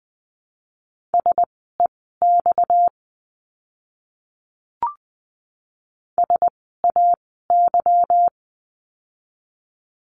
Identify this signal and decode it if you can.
Morse code, heard as a keyed tone, about 20 wpm, spaced as for 14 wpm, 700 Hz. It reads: SEX SAY
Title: 230 Lesson 37f - Letter x - Words 20wpm